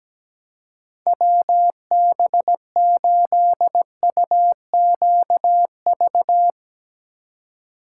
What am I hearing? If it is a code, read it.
Morse code, heard as a keyed tone, about 18 wpm, 700 Hz. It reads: WB8UQV
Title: Speed Racing - USA 2x3 Call Signs 17wpm